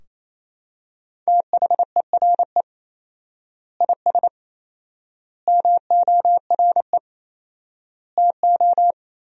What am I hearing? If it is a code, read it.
Morse code, heard as a keyed tone, about 28 wpm, 700 Hz. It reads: THERE IS MORE TO
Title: 4x Word-Spacing - Sentences from Top 200 Words - 28wpm